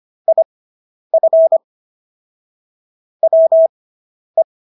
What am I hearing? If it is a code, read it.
Morse code, heard as a keyed tone, about 25 wpm, spaced as for 10 wpm, 650 Hz. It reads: IF WE